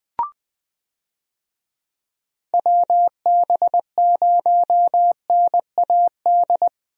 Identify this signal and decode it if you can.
Morse code, heard as a keyed tone, about 20 wpm, 700 Hz. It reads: WB0NAD
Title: USA Call Signs - 6 Characters - 20wpm